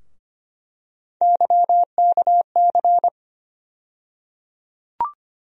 Morse code, heard as a keyed tone, about 25 wpm, 700 Hz. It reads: YKC
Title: USA Call Sign Postfixes - 3 Letters 25wpm